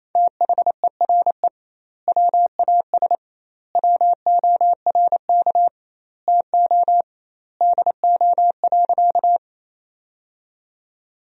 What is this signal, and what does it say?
Morse code, heard as a keyed tone, about 28 wpm, 700 Hz. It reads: THERE WAS WORK TO DO.
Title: 2x Word-Spacing - Sentences from Top 200 Words - 28wpm